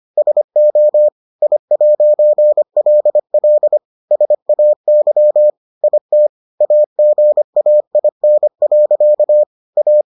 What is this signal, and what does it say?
Morse code, heard as a keyed tone, about 25 wpm, 600 Hz. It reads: SO I'LL SAY IT AGAIN. A